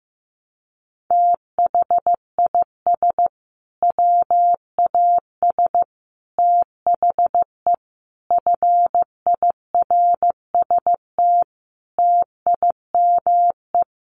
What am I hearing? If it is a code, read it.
Morse code, heard as a keyed tone, about 15 wpm, 700 Hz. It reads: THIS WAS THE FIRST TIME
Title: Sets of 5 Words Encore 15wpm